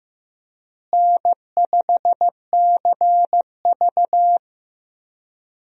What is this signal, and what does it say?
Morse code, heard as a keyed tone, about 15 wpm, 700 Hz. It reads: N5CV
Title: USA Call Signs - 4 Characters - Repeated 3x - 15wpm